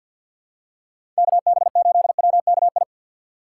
Morse code, heard as a keyed tone, about 47 wpm, 700 Hz. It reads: KB8WKI
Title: USA Call Signs - 6 Characters - 4x Character Spacing - 50wpm